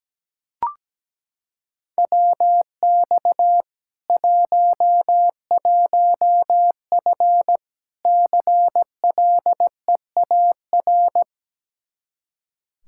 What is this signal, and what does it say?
Morse code, heard as a keyed tone, about 17 wpm, 700 Hz. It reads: WX 11F CLEAR
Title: Practice QSO - Exchange 2 - Info - 17wpm